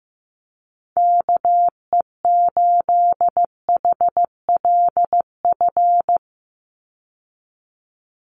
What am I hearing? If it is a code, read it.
Morse code, heard as a keyed tone, about 15 wpm, 700 Hz. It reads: KE8HLF
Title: USA Call Signs 15wpm